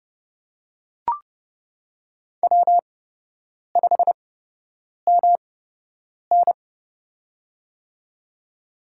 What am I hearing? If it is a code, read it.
Morse code, heard as a keyed tone, about 29 wpm, spaced as for 8 wpm, 700 Hz. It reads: W5MN
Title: USA Call Signs - 4 Characters - 8x Character Spacing - 30wpm